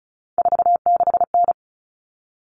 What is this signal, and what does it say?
Morse code, heard as a keyed tone, about 34 wpm, 700 Hz. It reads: 46N
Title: Common Cut Numbers - 3-Digit 35wpm